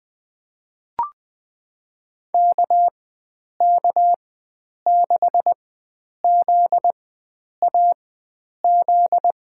Morse code, heard as a keyed tone, about 20 wpm, spaced as for 9 wpm, 700 Hz. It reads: KK6ZAZ
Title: USA Call Signs - 6 Characters - 4x Character Spacing - 20wpm